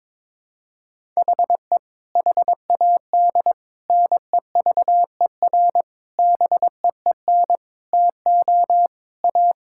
Morse code, heard as a keyed tone, about 22 wpm, 700 Hz. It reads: HE HAD NEVER BEEN TO A
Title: Sentences from Top 1000 Words - Review Part 5 - 22wpm